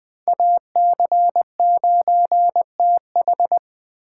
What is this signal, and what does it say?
Morse code, heard as a keyed tone, about 21 wpm, 700 Hz. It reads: AC9TH
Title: Speed Racing - USA 2x2 Call Signs 17wpm